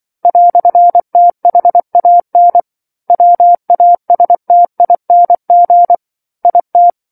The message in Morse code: "THAN WASTING IT